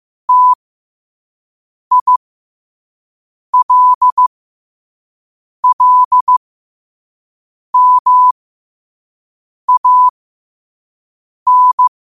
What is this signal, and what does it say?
Morse code, heard as a keyed tone, about 15 wpm, spaced as for 5 wpm, 1000 Hz. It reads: TILLMAN
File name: qotdfiles / 5wpm